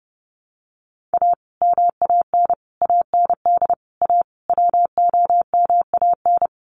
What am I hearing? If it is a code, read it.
Morse code, heard as a keyed tone, about 30 wpm, 700 Hz. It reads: A MAN AND A WOMAN
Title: Sets of 5 Words 30wpm